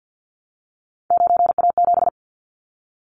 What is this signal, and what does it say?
Morse code, heard as a keyed tone, about 49 wpm, 700 Hz. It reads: 9A7
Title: Callsign Prefixes for Top 25 DXCC - 50wpm